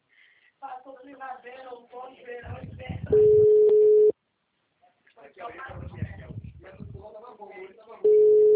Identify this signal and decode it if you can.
Ringback tone (Europe, 425 Hz), with speech.